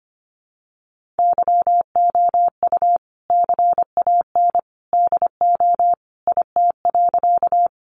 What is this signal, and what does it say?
Morse code, heard as a keyed tone, about 25 wpm, 700 Hz. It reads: YOU CAN DO IT.
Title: Sentences from Top 100 Words 25wpm